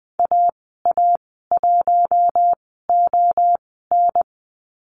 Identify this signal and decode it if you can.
Morse code, heard as a keyed tone, about 20 wpm, spaced as for 15 wpm, 700 Hz. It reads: AA1ON
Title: 048 Lesson 9g - Number 4 - Call Signs 20wpm